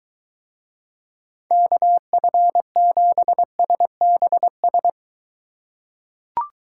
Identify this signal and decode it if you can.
Morse code, heard as a keyed tone, about 23 wpm, 700 Hz. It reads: KF7SBS